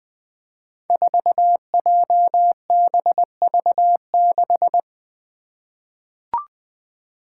Morse code, heard as a keyed tone, about 20 wpm, 700 Hz. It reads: 4JBV6
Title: Five Challenging Letter-Number Combinations 20wpm